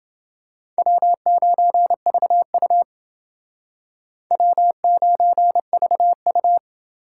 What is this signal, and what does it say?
Morse code, heard as a keyed tone, about 29 wpm, 700 Hz. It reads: W9VU W9VU